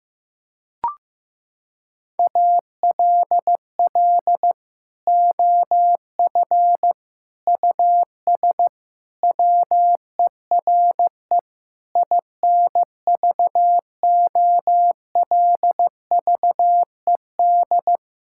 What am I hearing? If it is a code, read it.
Morse code, heard as a keyed tone, about 15 wpm, 700 Hz. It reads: ALL OF US WERE INVOLVED
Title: Sentences from Top 1000 Words - Review Part 5 - 15wpm